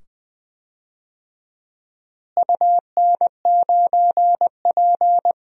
Morse code, heard as a keyed tone, about 20 wpm, 700 Hz. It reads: UN9P
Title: DXCC-100 Call Signs - 4 Characters - Repeated 2x - 20wpm